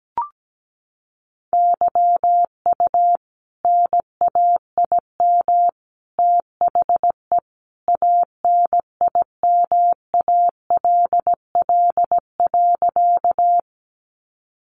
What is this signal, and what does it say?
Morse code, heard as a keyed tone, about 17 wpm, 700 Hz. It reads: YU NAIM THE ANIMALL.
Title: A Sound Of Thunder 17wpm